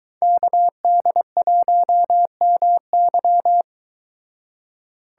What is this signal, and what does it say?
Morse code, heard as a keyed tone, about 22 wpm, 700 Hz. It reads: KD1MY